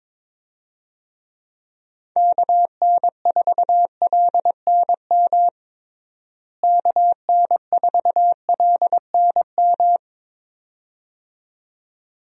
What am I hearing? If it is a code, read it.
Morse code, heard as a keyed tone, about 22 wpm, 700 Hz. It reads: KN4LNM KN4LNM
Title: USA Call Signs - 6 Characters - Repeated 2x - 22wpm